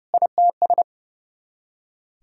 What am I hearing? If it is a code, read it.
Morse code, heard as a keyed tone, about 40 wpm, 700 Hz. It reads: ITS